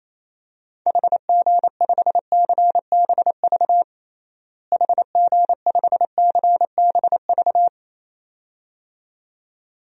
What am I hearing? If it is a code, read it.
Morse code, heard as a keyed tone, about 28 wpm, 700 Hz. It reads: HG5CBV HG5CBV
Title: DXCC-100 Call Signs - 6 Characters - Repeated 2x - 28wpm